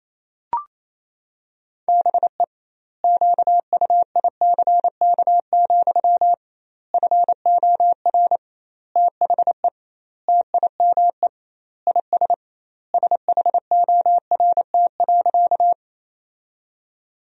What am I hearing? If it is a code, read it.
Morse code, heard as a keyed tone, about 28 wpm, 700 Hz. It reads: BE QUICK, FOR THE TIME IS SHORT.
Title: 2x Word-Spacing - Sentences from Top 500 Words - 28wpm